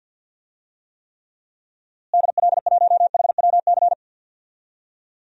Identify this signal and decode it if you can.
Morse code, heard as a keyed tone, about 52 wpm, 700 Hz. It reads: DL1HWC
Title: DXCC-100 Call Signs - 6 Characters - Repeated 3x - 50wpm